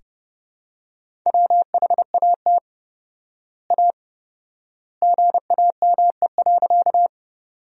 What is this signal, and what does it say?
Morse code, heard as a keyed tone, about 30 wpm, 700 Hz. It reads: WHAT A GAME.
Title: 4x Word-Spacing - Sentences from Top 500 Words - 30wpm